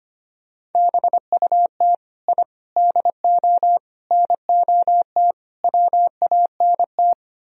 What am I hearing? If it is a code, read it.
Morse code, heard as a keyed tone, about 31 wpm, 700 Hz. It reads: BUT I DO NOT WANT
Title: Speed-Racing - Sets of 5 Words 25wpm